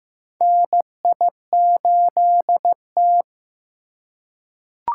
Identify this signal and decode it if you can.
Morse code, heard as a keyed tone, about 15 wpm, 700 Hz. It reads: NI8T
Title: USA Call Signs - 4 Characters - 3x Character Spacing - 15wpm